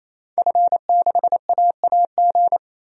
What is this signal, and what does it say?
Morse code, heard as a keyed tone, about 28 wpm, 700 Hz. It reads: F6AAG
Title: DXCC-100 Call Signs - 5 Characters - 28wpm